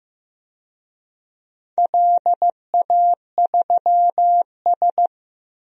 Morse code, heard as a keyed tone, about 15 wpm, 700 Hz. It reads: LA3S
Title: DXCC-100 Call Signs - 4 Characters - Repeated 2x - 15wpm